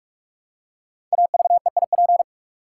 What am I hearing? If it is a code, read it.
Morse code, heard as a keyed tone, about 44 wpm, 700 Hz. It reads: AVEIP
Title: Code Groups - 5 Letters - 45wpm